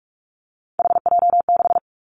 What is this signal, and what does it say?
Morse code, heard as a keyed tone, about 46 wpm, 700 Hz. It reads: HJ6
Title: Callsign Prefixes for Top 100 DXCC - 45wpm